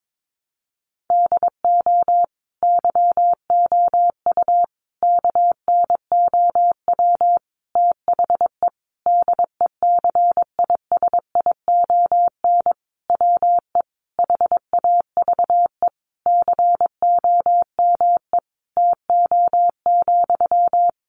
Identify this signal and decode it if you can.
Morse code, heard as a keyed tone, about 22 wpm, 700 Hz. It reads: DO YOU KNOW THE DECISION WE HAVE COME TO,
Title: Sentences from Top 1000 Words 22wpm